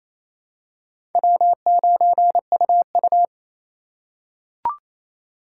Morse code, heard as a keyed tone, about 28 wpm, 700 Hz. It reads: W9UU
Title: USA Call Signs - 4 Characters - 2x Character Spacing - 28wpm